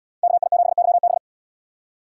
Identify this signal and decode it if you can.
Morse code, heard as a keyed tone, about 77 wpm, 700 Hz. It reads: KI7YB